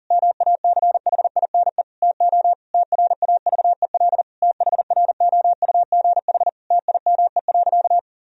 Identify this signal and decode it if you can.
Morse code, heard as a keyed tone, about 40 wpm, 700 Hz. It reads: MACHINE TO TRAVEL THROUGH TIME.
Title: Sentences from Top 1000 Words - Review Part 2 - 40wpm